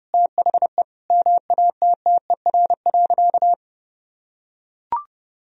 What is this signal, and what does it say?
Morse code, heard as a keyed tone, about 30 wpm, 700 Hz. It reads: THE MATTER.
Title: Sentences from Top 1000 Words - Review Part 1 - 30wpm